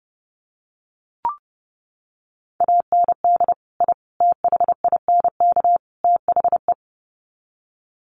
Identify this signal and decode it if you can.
Morse code, heard as a keyed tone, about 30 wpm, 700 Hz. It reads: AND I THINK THE
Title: Sets of 4 Words 30wpm